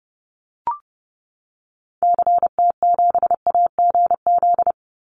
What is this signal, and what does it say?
Morse code, heard as a keyed tone, about 30 wpm, 700 Hz. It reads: CT7AGZ
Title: Call Signs - 30wpm